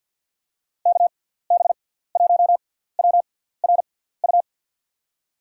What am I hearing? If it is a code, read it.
Morse code, heard as a keyed tone, about 51 wpm, spaced as for 17 wpm, 700 Hz. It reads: KB1WRU
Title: USA Call Signs - 6 Characters - 6x Character Spacing - 50wpm